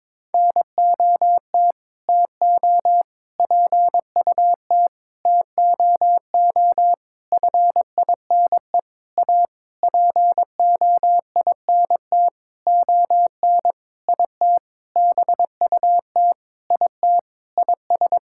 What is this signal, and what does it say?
Morse code, heard as a keyed tone, about 22 wpm, 700 Hz. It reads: NOT TO PUT TOO FINE A POINT ON IT BUT IT IS
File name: Sentences with 9-Word Phrases - 22wpm